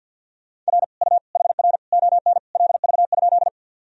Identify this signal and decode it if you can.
Morse code, heard as a keyed tone, about 50 wpm, 700 Hz. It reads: R U HR ON LV?
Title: Sentences from Top 400 Words and Abbreviations 50wpm